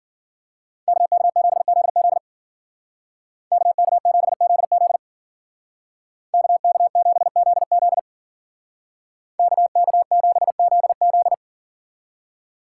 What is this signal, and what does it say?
Morse code, heard as a keyed tone, about 49 wpm, 700 Hz. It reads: KK7ZZ KK7ZZ KK7ZZ KK7ZZ
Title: Speed Racing - USA 2x2 Call Signs 40wpm